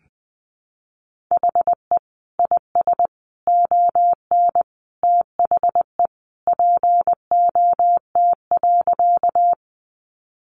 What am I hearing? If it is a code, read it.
Morse code, heard as a keyed tone, about 20 wpm, 700 Hz. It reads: HE IS ON THE POT.
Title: Sentences from Top 1000 Words 20wpm